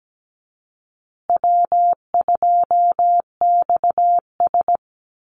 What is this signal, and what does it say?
Morse code, heard as a keyed tone, about 17 wpm, 700 Hz. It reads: W2XS
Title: Easy Call Signs Repeated 2x - 17wpm